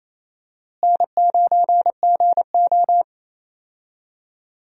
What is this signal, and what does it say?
Morse code, heard as a keyed tone, about 28 wpm, 700 Hz. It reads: N9GO
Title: USA Call Signs - 4 Characters - Repeated 3x - 28wpm